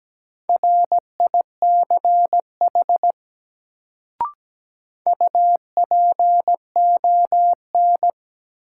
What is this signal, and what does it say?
Morse code, heard as a keyed tone, about 17 wpm, 700 Hz. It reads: RICH UPON
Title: Sentences from Top 1100 Words - New Word Drill Alt - 17wpm